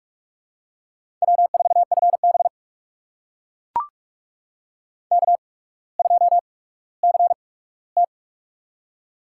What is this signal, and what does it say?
Morse code, heard as a keyed tone, about 44 wpm, spaced as for 19 wpm, 690 Hz. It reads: W4FB K2CT
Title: USA Call Signs - 4 Characters - 8x Character Spacing - 45wpm